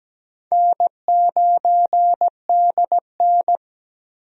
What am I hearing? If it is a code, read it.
Morse code, heard as a keyed tone, about 17 wpm, 700 Hz. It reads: N9DN
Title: USA Call Signs - 4 Characters - 17wpm